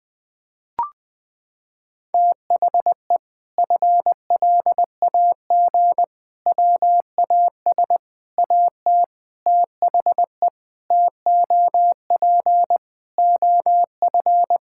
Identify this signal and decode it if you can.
Morse code, heard as a keyed tone, about 20 wpm, 700 Hz. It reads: THE FLAG WAS AT THE TOP OF
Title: Sentences with 5-Word Phrases - 20wpm